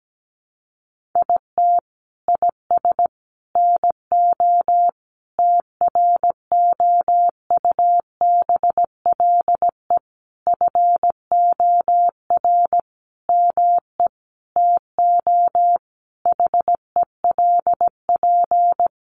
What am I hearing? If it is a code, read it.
Morse code, heard as a keyed tone, about 17 wpm, 700 Hz. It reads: IT IS NO TROUBLE FOR ME TO HELP
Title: Sentences from Top 1200 Words 17wpm